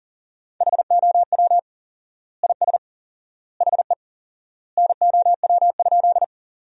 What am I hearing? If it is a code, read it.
Morse code, heard as a keyed tone, about 40 wpm, 700 Hz. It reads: HOW IS HE NOW?
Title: 4x Word-Spacing - Sentences from Top 100 Words - 40wpm